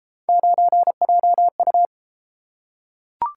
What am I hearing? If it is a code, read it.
Morse code, heard as a keyed tone, about 30 wpm, 700 Hz. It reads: 9JU